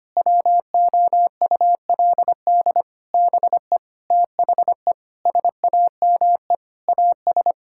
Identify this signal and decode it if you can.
Morse code, heard as a keyed tone, about 25 wpm, 700 Hz. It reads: WOULD BE THE SAME AS
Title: Sentences from Top 1000 Words - Review Part 5 - 25wpm